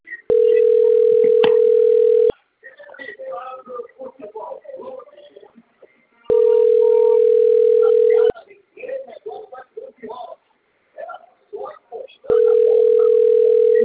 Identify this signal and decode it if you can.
Ringback tone (North America), 440 + 480 Hz, with speech.
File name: out- - - - - .52091